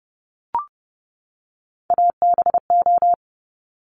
Morse code, heard as a keyed tone, about 30 wpm, 700 Hz. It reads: ABO
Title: Top Three Letter Combinations in Words 30wpm